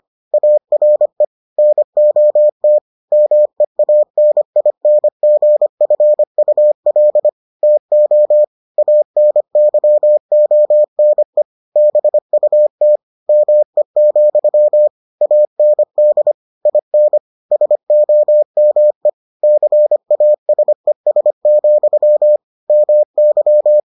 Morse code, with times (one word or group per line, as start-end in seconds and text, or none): 0.3-1.2 ARE
1.6-2.8 NOT
3.1-7.3 MEANINGFUL
7.6-8.4 TO
8.8-11.4 ANYONE
11.8-13.0 BUT
13.3-14.9 ME,
15.2-16.3 AND
16.7-17.2 IN
17.5-19.1 SOME
19.4-22.4 CASES,
22.7-23.8 MY